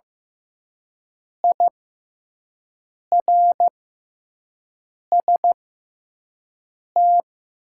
Morse code, heard as a keyed tone, about 15 wpm, spaced as for 5 wpm, 700 Hz. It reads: IRST
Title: 6x Character Spacing - Top 200 Words - 15wpm